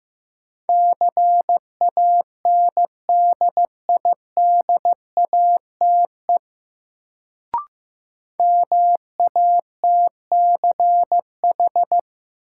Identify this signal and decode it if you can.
Morse code, heard as a keyed tone, about 15 wpm, 700 Hz. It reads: CANDIDATE MATCH
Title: Sentences from Top 1600 Words - New Word Drill - 15wpm